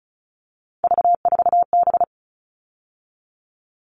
Morse code, heard as a keyed tone, about 34 wpm, 700 Hz. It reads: V4B
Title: All Cut Numbers - 3-Digit 35wpm